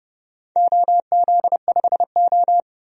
Morse code, heard as a keyed tone, about 30 wpm, 700 Hz. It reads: OZ5O
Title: DXCC-100 Call Signs - 4 Characters - 30wpm